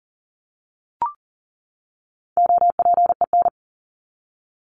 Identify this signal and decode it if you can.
Morse code, heard as a keyed tone, about 38 wpm, 700 Hz. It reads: OPEN
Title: ING Suffix - Top 100 Words with Base Words 40wpm